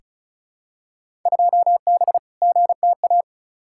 Morse code, heard as a keyed tone, about 34 wpm, 700 Hz. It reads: 2B GTA
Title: Field Day Exchanges Encore - 35wpm